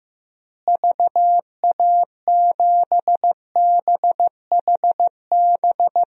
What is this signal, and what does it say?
Morse code, heard as a keyed tone, about 15 wpm, 700 Hz. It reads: VA7BHB
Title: DXCC-100 Call Signs - 6 Characters - Repeated 3x - 15wpm